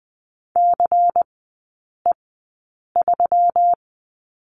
Morse code, heard as a keyed tone, about 20 wpm, spaced as for 8 wpm, 700 Hz. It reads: CE3
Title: 127 Lesson 21e - Number 6 - Three Characters 20wpm